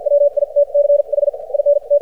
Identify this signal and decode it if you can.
Morse code, heard as a keyed tone, about 51 wpm, spaced as for 44 wpm, 590 Hz. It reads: WITCH IN T
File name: archived / CRSnet